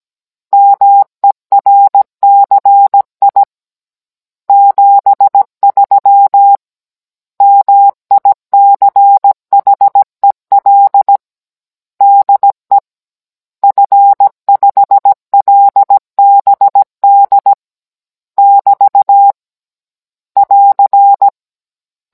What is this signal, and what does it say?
Morse code, heard as a keyed tone, about 17 wpm, 800 Hz. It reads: MERCI 73 MICHEL DE F5LBD <BT> <AR>
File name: Cours-NR19-16mm